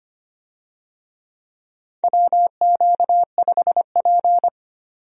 Morse code, heard as a keyed tone, about 25 wpm, 700 Hz. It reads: WQ5P
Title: USA Call Signs - 4 Characters - 25wpm